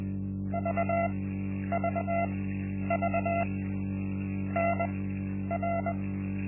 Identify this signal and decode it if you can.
Morse code, heard as a keyed tone, about 21 wpm, spaced as for 10 wpm, 680 Hz. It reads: VVV NR
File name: Linksmakalnis-1